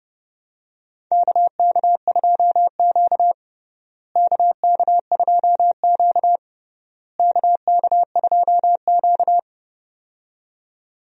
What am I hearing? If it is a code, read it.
Morse code, heard as a keyed tone, about 29 wpm, 700 Hz. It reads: KK2Q KK2Q KK2Q
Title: USA Call Signs - 4 Characters - Repeated 3x - 30wpm